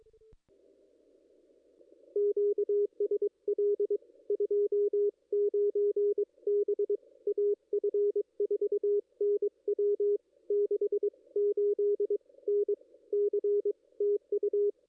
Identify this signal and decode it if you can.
Morse code, heard as a keyed tone, about 23 wpm, spaced as for 20 wpm, 440 Hz. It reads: QSL 29 B AF4NW 68 NC TU
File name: contests / ss-cw-k5tr-op-n5rz -2200